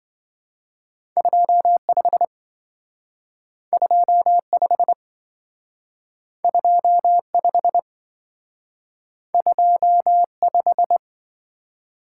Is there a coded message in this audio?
Yes: Morse code from 1.2 to 11.0 s, heard as a keyed tone, about 26 wpm, 700 Hz. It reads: 25 25 25 25